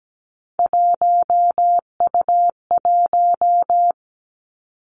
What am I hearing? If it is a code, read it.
Morse code, heard as a keyed tone, about 17 wpm, 700 Hz. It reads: 1U1
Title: Three Challenging Letter-Number Combinations 17wpm